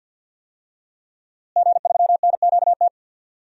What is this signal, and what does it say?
Morse code, heard as a keyed tone, about 52 wpm, 700 Hz. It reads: G3NQT
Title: DXCC-100 Call Signs - 5 Characters - Repeated 2x - 50wpm